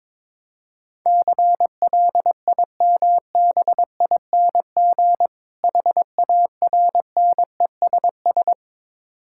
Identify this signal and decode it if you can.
Morse code, heard as a keyed tone, about 22 wpm, 690 Hz. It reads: CLIMBING HARNESS
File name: Difficult Words with Hints - 22wpm